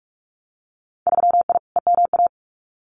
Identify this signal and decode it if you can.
Morse code, heard as a keyed tone, about 44 wpm, 700 Hz. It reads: 3I EMA
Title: Winter Field Day Exchanges Encore 45wpm